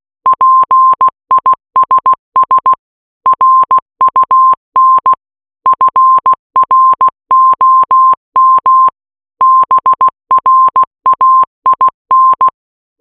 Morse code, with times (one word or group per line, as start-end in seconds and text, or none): 0.3-2.7 PISS
3.3-5.1 RUN
5.7-8.9 FROM
9.4-12.5 BRAIN